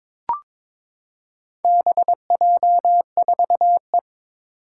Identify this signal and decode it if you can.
Morse code, heard as a keyed tone, about 22 wpm, 700 Hz. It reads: BJ4E
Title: Four Challenging Letter-Number Combinations 22wpm